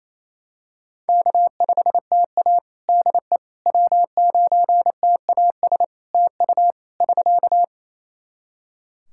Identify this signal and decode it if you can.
Morse code, heard as a keyed tone, about 28 wpm, 700 Hz. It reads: K5TA DE W9TAS TU <SK>
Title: Practice QSO - Exchange 4 - Ending - 28wpm